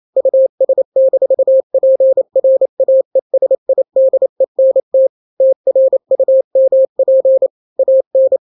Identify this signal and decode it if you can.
Morse code, heard as a keyed tone, about 28 wpm, 530 Hz. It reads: US-PRAESIDENT TRUMP AN
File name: daily / 27wpm